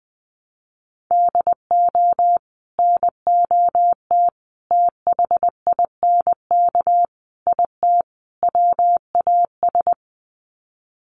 Morse code, heard as a keyed tone, about 20 wpm, 700 Hz. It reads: DO NOT THINK IT WAS
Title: Sets of 5 Words Encore 20wpm